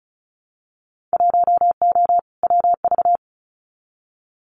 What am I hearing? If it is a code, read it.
Morse code, heard as a keyed tone, about 36 wpm, 700 Hz. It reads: 1O WV
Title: Winter Field Day Exchanges Encore 35wpm